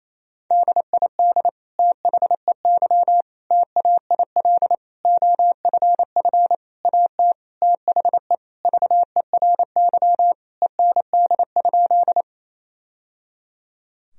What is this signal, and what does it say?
Morse code, heard as a keyed tone, about 28 wpm, 700 Hz. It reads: DID THEY TAIL OFF AT THE VERY END?
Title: Sentences from Top 1000 Words - Review Part 3 - 28wpm